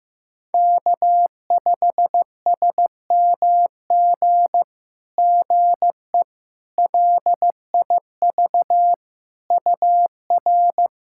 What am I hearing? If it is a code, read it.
Morse code, heard as a keyed tone, about 15 wpm, 700 Hz. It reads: K5SMG GE LIV UR
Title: SOTA QSO Practice 15wpm